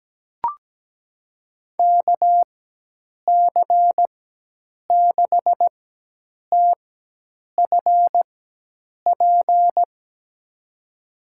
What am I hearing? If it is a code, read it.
Morse code, heard as a keyed tone, about 17 wpm, spaced as for 8 wpm, 700 Hz. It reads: KC6TFP
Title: USA Call Signs - 6 Characters - 4x Character Spacing - 17wpm